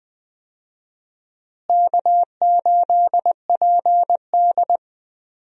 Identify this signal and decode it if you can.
Morse code, heard as a keyed tone, about 20 wpm, 700 Hz. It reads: K8PD
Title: USA Call Signs - 4 Characters - Repeated 3x - 20wpm